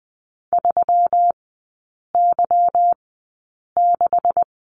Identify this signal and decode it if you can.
Morse code, heard as a keyed tone, about 20 wpm, spaced as for 8 wpm, 700 Hz. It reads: 3Y6
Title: 184 Lesson 30e - Letter b - Three Characters 20wpm